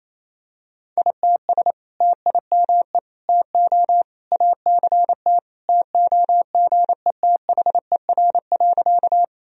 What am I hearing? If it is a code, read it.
Morse code, heard as a keyed tone, about 28 wpm, 700 Hz. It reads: ITS TIME TO ACT TOGETHER.
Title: Sentences from Top 1000 Words - Review Part 1 - 28wpm